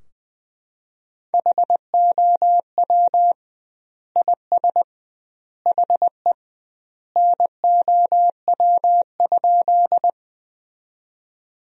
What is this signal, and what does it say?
Morse code, heard as a keyed tone, about 20 wpm, 700 Hz. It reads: HOW IS HE NOW?
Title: 2x Word-Spacing - Sentences from Top 100 Words - 20wpm